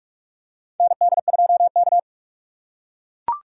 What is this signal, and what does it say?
Morse code, heard as a keyed tone, about 44 wpm, 690 Hz. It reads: ND2K
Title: USA Call Signs - 4 Characters - 3x Character Spacing - 45wpm